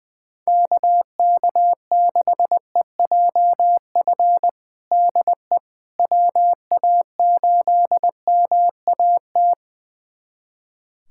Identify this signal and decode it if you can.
Morse code, heard as a keyed tone, about 21 wpm, 700 Hz. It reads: KK6EJF DE WA8MAT
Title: USA Callsign DE Callsign 20wpm